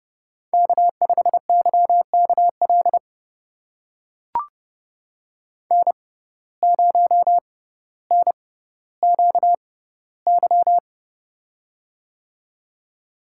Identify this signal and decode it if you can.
Morse code, heard as a keyed tone, about 29 wpm, spaced as for 15 wpm, 700 Hz. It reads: K5YKL N0NQY
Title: USA Call Signs - 5 Characters - 6x Character Spacing - 30wpm